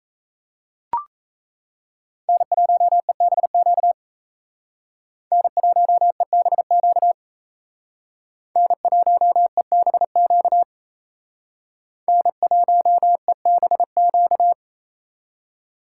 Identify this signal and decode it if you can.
Morse code, heard as a keyed tone, about 34 wpm, 700 Hz. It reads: N1EBQ N1EBQ N1EBQ N1EBQ
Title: Speed Racing - USA 1x3 Call Signs 28wpm